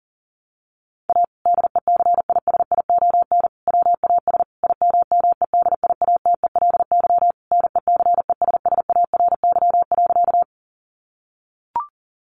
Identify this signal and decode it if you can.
Morse code, heard as a keyed tone, about 40 wpm, 700 Hz. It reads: A DECISION WAS IMMEDIATELY NECESSARY.
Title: Sentences from Top 1100 Words 40wpm